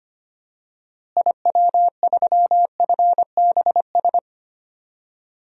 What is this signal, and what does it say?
Morse code, heard as a keyed tone, about 25 wpm, 700 Hz. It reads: IW3FBS
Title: DXCC-100 Call Signs - 6 Characters - Repeated 2x - 25wpm